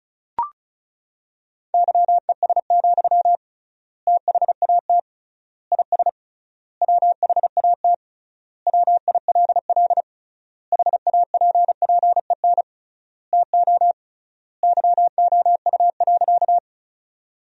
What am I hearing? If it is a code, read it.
Morse code, heard as a keyed tone, about 36 wpm, 700 Hz. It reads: YES, THAT IS WHAT WILL HAPPEN TO YOU.
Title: 3x Word-Spacing - Sentences from Top 500 Words - 35wpm